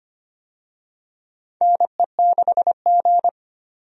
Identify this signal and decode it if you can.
Morse code, heard as a keyed tone, about 25 wpm, 700 Hz. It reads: NE6G
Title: USA Call Signs - 4 Characters - 25wpm